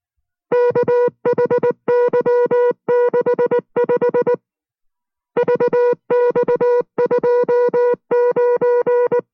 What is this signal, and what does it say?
Morse code, heard as a keyed tone, about 19 wpm, 490 Hz. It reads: KHY65 VX29